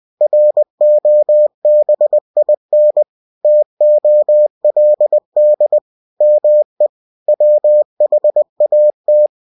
Morse code, heard as a keyed tone, about 20 wpm, 600 Hz. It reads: ROBIN TOLD ME WHAT
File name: cw-books / winnie-the-pooh / winnie-20wpm-page-0052